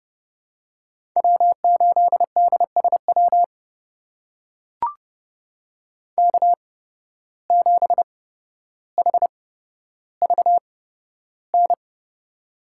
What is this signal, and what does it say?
Morse code, heard as a keyed tone, about 30 wpm, spaced as for 13 wpm, 700 Hz. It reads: W8DSW K7HVN
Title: USA Call Signs - 5 Characters - 8x Character Spacing - 30wpm